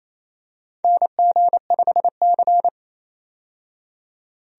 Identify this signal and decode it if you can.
Morse code, heard as a keyed tone, about 28 wpm, 700 Hz. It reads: NG5C
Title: USA Call Signs - 4 Characters - 28wpm